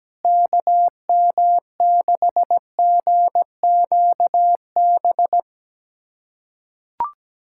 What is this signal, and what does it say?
Morse code, heard as a keyed tone, about 17 wpm, 700 Hz. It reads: KM6GQB
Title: USA Callsign DE Callsign 17wpm